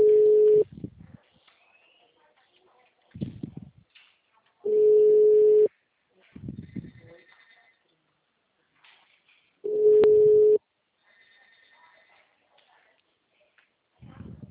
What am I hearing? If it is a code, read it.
Ringback tone (Europe, 425 Hz).
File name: out- -2006- - - .28844